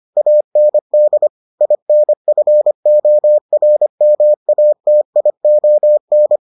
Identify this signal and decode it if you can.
Morse code, heard as a keyed tone, about 25 wpm, 600 Hz. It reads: AND INFORMATION